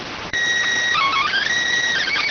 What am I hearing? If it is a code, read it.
SSTV (slow-scan television), PD 290.